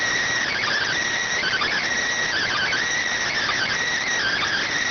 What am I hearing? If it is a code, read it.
Emergency Alert System (SAME) tones, heard as harsh buzzing bursts. It is heard from 0.5 to 4.7 s.
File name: findings / CT AXIAL -572mm 2.gen-38 rx